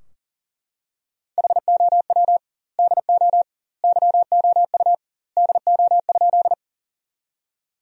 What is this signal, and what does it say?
Morse code, heard as a keyed tone, about 40 wpm, 700 Hz. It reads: HOW DO YOU DO?
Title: 2x Word-Spacing - Sentences from Top 100 Words - 40wpm